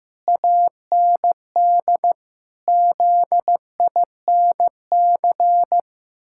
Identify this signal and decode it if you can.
Morse code, heard as a keyed tone, about 14 wpm, 700 Hz. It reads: AND ZINC
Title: Pangram Challenge v2 - 15wpm